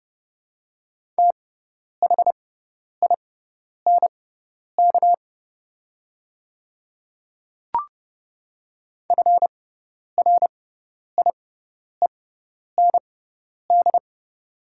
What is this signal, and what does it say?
Morse code, heard as a keyed tone, about 30 wpm, spaced as for 10 wpm, 700 Hz. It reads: THINK FRIEND
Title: 6x Character Spacing - Top 300 Words - 30wpm